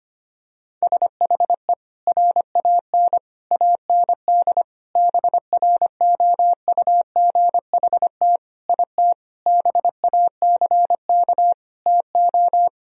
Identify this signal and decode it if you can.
Morse code, heard as a keyed tone, about 25 wpm, 700 Hz. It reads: SHE RAN AND BROUGHT IT BACK TO
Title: Sentences from Top 1000 Words - Review Part 3 - 25wpm